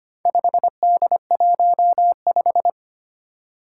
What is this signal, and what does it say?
Morse code, heard as a keyed tone, about 25 wpm, 700 Hz. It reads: 5D15
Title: Four Challenging Letter-Number Combinations 25wpm